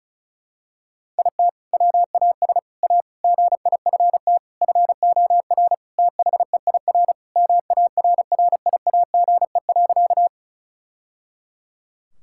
Morse code, heard as a keyed tone, about 34 wpm, 700 Hz. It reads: IT WAS A GIFT FOR THEIR MARRIAGE.
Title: Sentences from Top 1000 Words - Review Part 6 - 35wpm